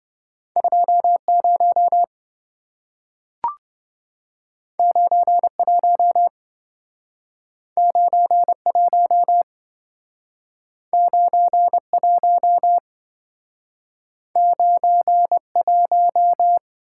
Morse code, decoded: 20 91 91 91 91

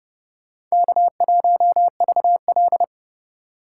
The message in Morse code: K1VL

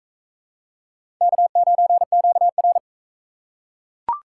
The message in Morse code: K9QR

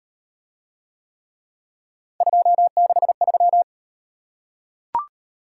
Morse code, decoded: 263

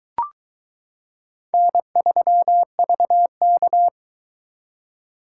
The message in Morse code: N3VK

